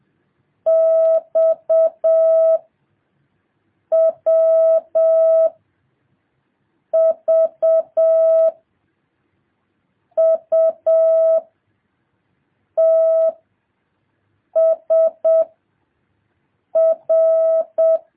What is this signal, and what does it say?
Morse code, heard as a keyed tone, about 7 wpm, 650 Hz. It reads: X W V U T S R
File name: jh8cbh / 2023cw